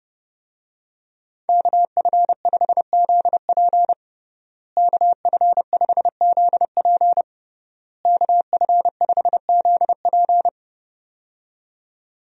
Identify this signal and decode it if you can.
Morse code, heard as a keyed tone, about 30 wpm, 700 Hz. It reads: KF5ZP KF5ZP KF5ZP
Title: USA Call Signs - 5 Characters - Repeated 3x - 30wpm